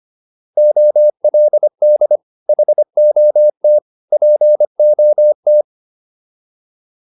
Morse code, heard as a keyed tone, about 25 wpm, 600 Hz. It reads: OLD HOT POT